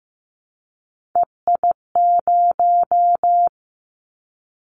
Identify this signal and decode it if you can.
Morse code, heard as a keyed tone, about 15 wpm, 700 Hz. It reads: EI0
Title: Callsign Prefixes for Top 50 DXCC - 15wpm